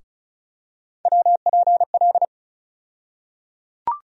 Morse code, heard as a keyed tone, about 36 wpm, 700 Hz. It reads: WPL
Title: USA Call Sign Postfixes - 3 Letters 35wpm